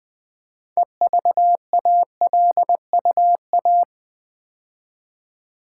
Morse code, heard as a keyed tone, about 20 wpm, 700 Hz. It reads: EVALUA
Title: Build A Word - 3 Letters At A Time - 8-Letter Words - 20wpm